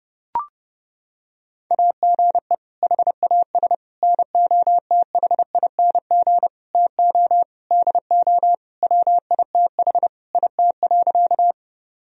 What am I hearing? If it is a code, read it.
Morse code, heard as a keyed tone, about 32 wpm, 700 Hz. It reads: AGE HAS NOTHING TO DO WITH IT.
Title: Sentences from Top 1000 Words - Review Part 3 - 30wpm